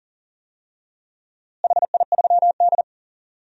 Morse code, heard as a keyed tone, about 40 wpm, 700 Hz. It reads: HI3D
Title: DXCC-100 Call Signs - 4 Characters - Repeated 3x - 40wpm